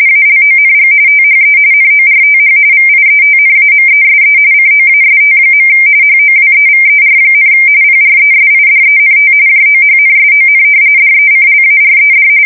RTTY, about 45.45 baud. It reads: ALONG THE NORTH VIETNAMESE COAST. HOWEVER, PRESIDENT JOHNSON DECLARED
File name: fsk170 / battleshipnj